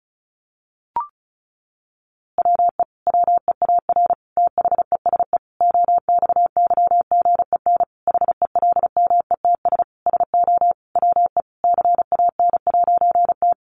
Morse code, heard as a keyed tone, about 34 wpm, 700 Hz. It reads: WE WEAR THESE OXYGEN HELMETS SO WE CAN'T
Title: A Sound Of Thunder 35wpm